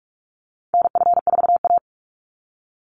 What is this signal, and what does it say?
Morse code, heard as a keyed tone, about 44 wpm, 700 Hz. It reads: NF4A
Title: Call Signs - 45wpm